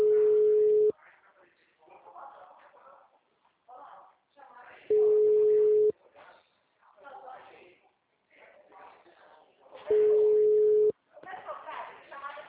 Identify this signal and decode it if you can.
Ringback tone (Europe, 425 Hz), with speech.